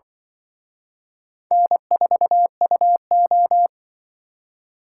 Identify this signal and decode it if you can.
Morse code, heard as a keyed tone, about 23 wpm, 700 Hz. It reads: N4UO
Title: Speed Racing - USA 1x2 Call Signs 20wpm